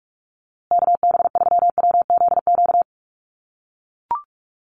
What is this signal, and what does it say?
Morse code, heard as a keyed tone, about 44 wpm, 700 Hz. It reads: KB3WZQ
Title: USA Call Signs Repeated 3x 45wpm